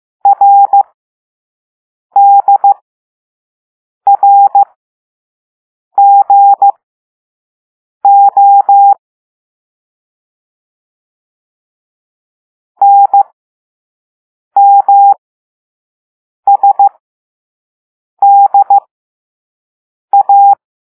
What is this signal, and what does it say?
Morse code, heard as a keyed tone, about 15 wpm, spaced as for 5 wpm, 800 Hz. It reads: RDRGO NMSDA